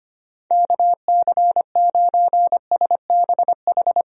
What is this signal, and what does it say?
Morse code, heard as a keyed tone, about 25 wpm, 700 Hz. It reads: KC9SBH
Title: USA Call Signs - 6 Characters - 25wpm